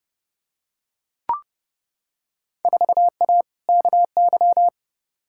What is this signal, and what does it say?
Morse code, heard as a keyed tone, about 31 wpm, 700 Hz. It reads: 4A KY
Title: Field Day Exchanges Encore - 30wpm